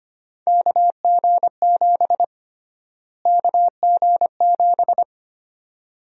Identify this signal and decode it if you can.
Morse code, heard as a keyed tone, about 25 wpm, 700 Hz. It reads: KG7 KG7
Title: USA Call Sign Prefixes - Repeated 2x - 25wpm